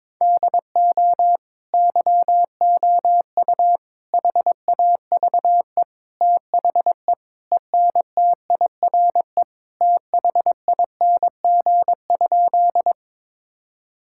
Morse code, heard as a keyed tone, about 22 wpm, 700 Hz. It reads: DO YOU HAVE THE ENTIRE THING?
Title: Sentences from Top 1000 Words - Review Part 5 - 22wpm